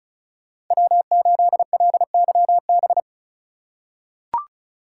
Morse code, heard as a keyed tone, about 36 wpm, 700 Hz. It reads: W8LYB
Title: USA Call Signs - 5 Characters - 2x Character Spacing - 35wpm